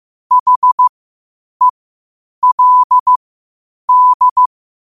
Morse code, heard as a keyed tone, about 15 wpm, spaced as for 8 wpm, 1000 Hz. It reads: HELD
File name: qotdfiles / 8wpm